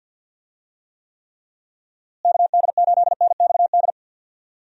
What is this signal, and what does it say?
Morse code, heard as a keyed tone, about 51 wpm, 700 Hz. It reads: KD8NXD
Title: USA Call Signs - 6 Characters - Repeated 3x - 50wpm